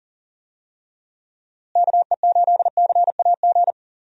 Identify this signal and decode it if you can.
Morse code, heard as a keyed tone, about 39 wpm, 700 Hz. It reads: KE8CAG
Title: USA Call Signs - 6 Characters - Repeated 2x - 40wpm